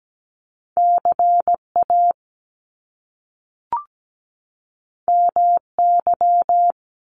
Morse code, heard as a keyed tone, about 17 wpm, 700 Hz. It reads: CA MY
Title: Top Two Letter Combinations in Words 17wpm